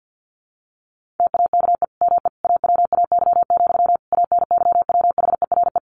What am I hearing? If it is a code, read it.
Morse code, heard as a keyed tone, about 50 wpm, 700 Hz. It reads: TAKE ME AWAY, ANYWHERE
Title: Sentences from Top 1400 Words 50wpm